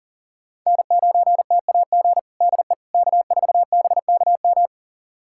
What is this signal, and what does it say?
Morse code, heard as a keyed tone, about 40 wpm, 700 Hz. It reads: N9TAG DE K4BKM
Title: USA Callsign DE Callsign - 5 Characters - 40wpm